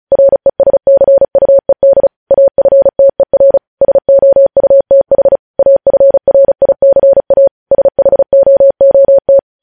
Morse code, heard as a keyed tone, about 36 wpm, 550 Hz. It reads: RESCUED AFTER SOUTH AFRICA SHOOT
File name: podcast / News headlines-1x-35x35